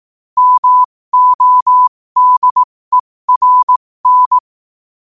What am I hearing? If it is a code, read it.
Morse code, heard as a keyed tone, about 18 wpm, spaced as for 15 wpm, 1000 Hz. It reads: MODERN
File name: 15wpm